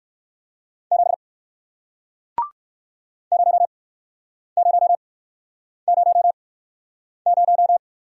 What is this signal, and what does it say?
Morse code, heard as a keyed tone, about 60 wpm, spaced as for 9 wpm, 700 Hz. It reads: 7 0000